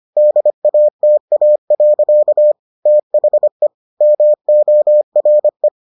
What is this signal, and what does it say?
Morse code, heard as a keyed tone, about 25 wpm, 600 Hz. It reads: DATA. THE MORE